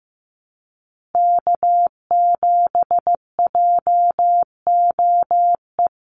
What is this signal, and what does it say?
Morse code, heard as a keyed tone, about 15 wpm, 700 Hz. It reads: K7JOE
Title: Call Signs Repeated 2x - 15wpm